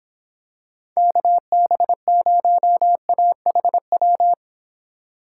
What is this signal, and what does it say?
Morse code, heard as a keyed tone, about 25 wpm, 700 Hz. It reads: KB0AHW